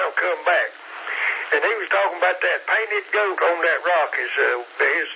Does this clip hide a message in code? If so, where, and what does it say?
Yes: Emergency Alert System (SAME) tones from 1.0 to 4.5 s, heard as harsh buzzing bursts, with speech.